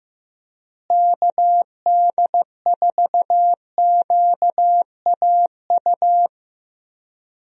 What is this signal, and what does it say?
Morse code, heard as a keyed tone, about 15 wpm, 700 Hz. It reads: KD4QAU
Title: USA Call Signs - 6 Characters - Repeated 3x - 15wpm